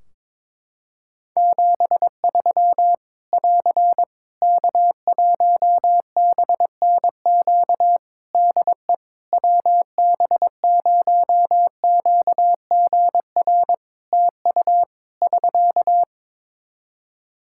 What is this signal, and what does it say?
Morse code, heard as a keyed tone, about 22 wpm, 700 Hz. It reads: 73 <AR> K1BNQ DE WB0QGR TU <SK>
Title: Practice QSO - Exchange 4 - Ending - 22wpm